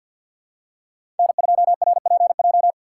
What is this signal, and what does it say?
Morse code, heard as a keyed tone, about 46 wpm, 690 Hz. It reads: N2RPJ
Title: USA Call Signs - 5 Characters - 2x Character Spacing - 50wpm